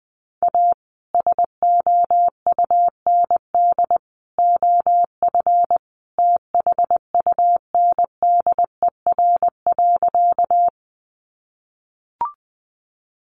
Morse code, heard as a keyed tone, about 20 wpm, 700 Hz. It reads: A SOUND OF THUNDER.
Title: A Sound Of Thunder 20wpm